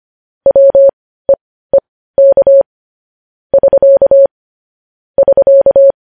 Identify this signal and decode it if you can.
Morse code, heard as a keyed tone, about 25 wpm, spaced as for 15 wpm, 550 Hz. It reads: WEEK <SK> <SK>
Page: news-headlines-in-morse-code-at-15-wpm